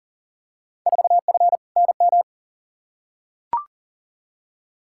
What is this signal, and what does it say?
Morse code, heard as a keyed tone, about 40 wpm, 700 Hz. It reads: 4F NM
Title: Field Day Exchanges Encore - 40wpm